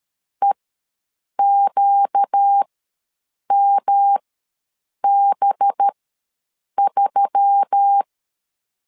Morse code, heard as a keyed tone, about 13 wpm, spaced as for 7 wpm, 800 Hz. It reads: EQMB3
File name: 5 LB ws 12wpm